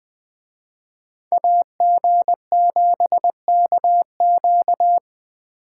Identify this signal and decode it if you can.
Morse code, heard as a keyed tone, about 20 wpm, 700 Hz. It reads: AG7KQ